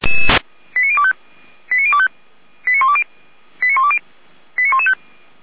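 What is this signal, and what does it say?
Five-tone selective call (ZVEI), heard as quick runs of five tones, tones of about 70 ms. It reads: 89025 89025 89120 89120 89115
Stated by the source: ZVEI five-tone call received on SDR.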